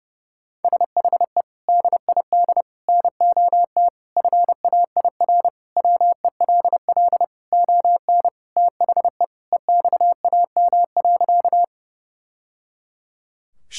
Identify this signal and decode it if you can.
Morse code, heard as a keyed tone, about 30 wpm, 700 Hz. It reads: SHE DID NOT FAIR WELL ON THE EXAM.
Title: Sentences from Top 1000 Words - Review Part 5 - 30wpm